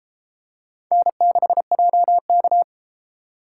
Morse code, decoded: N6JK

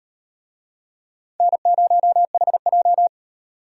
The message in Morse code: N0HJ

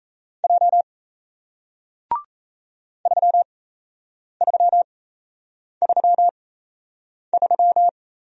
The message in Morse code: J 3333

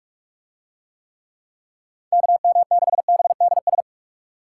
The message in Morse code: KM6BDS

